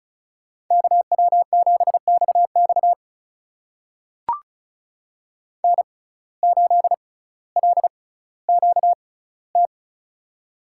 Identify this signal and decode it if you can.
Morse code, heard as a keyed tone, about 36 wpm, spaced as for 18 wpm, 700 Hz. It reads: KW7XX N8LQT
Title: USA Call Signs - 5 Characters - 6x Character Spacing - 35wpm